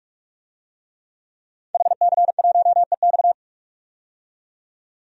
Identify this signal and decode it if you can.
Morse code, heard as a keyed tone, about 46 wpm, 700 Hz. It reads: HC1EX
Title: DXCC-100 Call Signs - 5 Characters - 45wpm